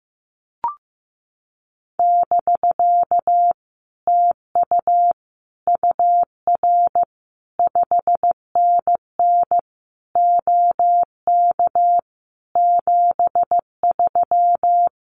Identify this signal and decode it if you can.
Morse code, heard as a keyed tone, about 15 wpm, 700 Hz. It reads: <BK> TU UR 5NN OK 73
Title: Common QSO Phrases Encore 15wpm